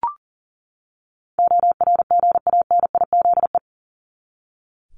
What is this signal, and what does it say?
Morse code, heard as a keyed tone, about 40 wpm, 700 Hz. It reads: ORGANIZE
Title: Top 1751-2000 Words 40wpm